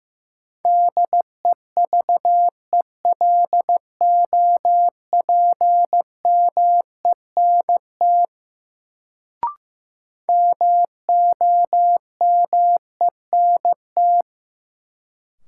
Morse code, heard as a keyed tone, about 15 wpm, 700 Hz. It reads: DEVELOPMENT MOMENT
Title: Sets of 3 Words - New Word Drill - 15wpm